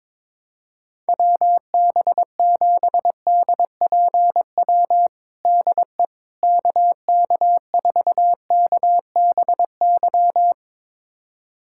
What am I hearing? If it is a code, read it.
Morse code, heard as a keyed tone, about 22 wpm, 700 Hz. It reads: WB7DPW DE KK4KBY
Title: USA Callsign DE Callsign - 6 Characters - 22wpm